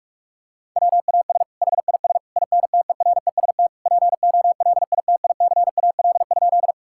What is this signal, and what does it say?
Morse code, heard as a keyed tone, about 46 wpm, 700 Hz. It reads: WAS HIS INTEREST POLITICAL?
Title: Sentences from Top 1000 Words - Review Part 5 - 45wpm